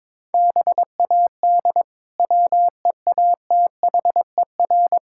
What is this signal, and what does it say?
Morse code, heard as a keyed tone, about 22 wpm, 700 Hz. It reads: BAD WEATHER
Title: Sentences from Top 1000 Words - Review Part 5 - 22wpm